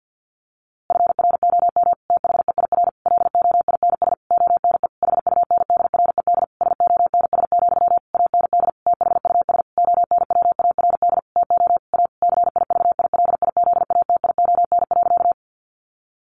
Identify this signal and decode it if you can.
Morse code, heard as a keyed tone, about 48 wpm, 700 Hz. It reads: FROM THEIR LOINS ONE HUNDRED SONS, AND THUS ONWARD TO A CIVILIZATION.
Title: A Sound Of Thunder 50wpm